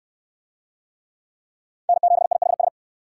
Morse code, heard as a keyed tone, about 69 wpm, 700 Hz. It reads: N7IHS